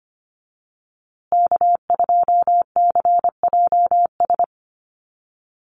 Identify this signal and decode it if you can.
Morse code, heard as a keyed tone, about 25 wpm, 700 Hz. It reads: K2CJS
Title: USA Call Signs Repeated 3x 25wpm